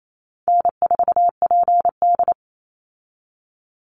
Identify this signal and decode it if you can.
Morse code, heard as a keyed tone, about 28 wpm, 700 Hz. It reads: N4PD
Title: Easy Call Signs - 28wpm